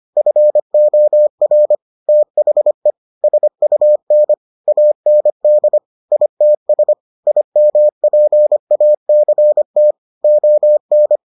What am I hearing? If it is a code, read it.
Morse code, heard as a keyed tone, about 25 wpm, 600 Hz. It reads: FOR THE SUN AND ITS IMPACT ON